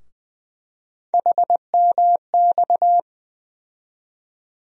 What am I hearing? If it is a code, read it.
Morse code, heard as a keyed tone, about 20 wpm, 700 Hz. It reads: HMX
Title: USA Call Sign Postfixes - 3 Letters 20wpm